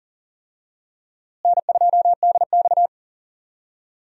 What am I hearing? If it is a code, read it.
Morse code, heard as a keyed tone, about 40 wpm, 700 Hz. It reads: N2DX
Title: USA Call Signs - 4 Characters - Repeated 3x - 40wpm